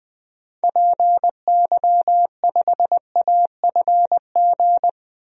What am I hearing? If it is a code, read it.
Morse code, heard as a keyed tone, about 21 wpm, 700 Hz. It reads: PY5AFG
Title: DXCC-100 Call Signs - 6 Characters - 20wpm